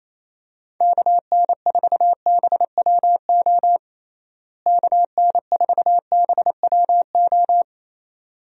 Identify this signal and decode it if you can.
Morse code, heard as a keyed tone, about 28 wpm, 700 Hz. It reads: KN4BWO KN4BWO
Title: USA Call Signs - 6 Characters - Repeated 2x - 28wpm